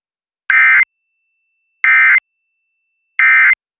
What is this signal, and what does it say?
Emergency Alert System (SAME) tones, heard as harsh buzzing bursts. It is heard from 0.5 to 3.5 s.